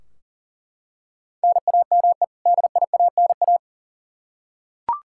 Morse code, heard as a keyed tone, about 37 wpm, 700 Hz. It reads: NAME DIANA
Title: Practice QSO - Exchange 1 - Beginning - 40wpm